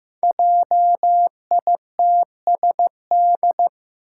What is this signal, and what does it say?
Morse code, heard as a keyed tone, about 15 wpm, 700 Hz. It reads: JITSD
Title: Five Challenging Letter-Number Combinations 15wpm